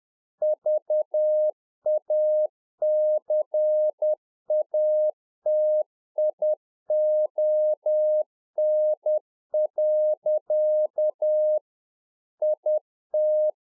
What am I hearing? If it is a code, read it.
Morse code, heard as a keyed tone, about 10 wpm, 610 Hz. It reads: VACATION. IT